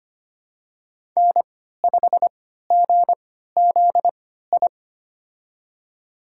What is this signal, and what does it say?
Morse code, heard as a keyed tone, about 25 wpm, spaced as for 14 wpm, 700 Hz. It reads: N5GZI
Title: USA Call Signs - 5 Characters - 3x Character Spacing - 25wpm